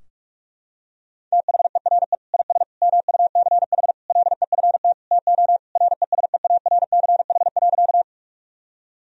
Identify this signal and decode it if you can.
Morse code, heard as a keyed tone, about 47 wpm, 700 Hz. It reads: THERE IS MUCH LEFT TO RESEARCH.
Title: Sentences from Top 1000 Words - Review Part 4 - 45wpm